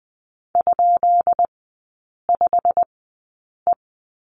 Morse code, heard as a keyed tone, about 20 wpm, spaced as for 8 wpm, 700 Hz. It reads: ?5E
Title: 133 Lesson 22e - Question Mark - Three Characters 20wpm